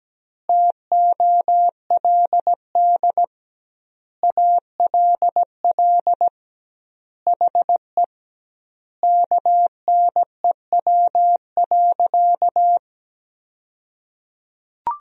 Morse code, heard as a keyed tone, about 17 wpm, 700 Hz. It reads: TOLD ALL HE KNEW.
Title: 2x Word-Spacing - Sentences from Top 400 Words - 17wpm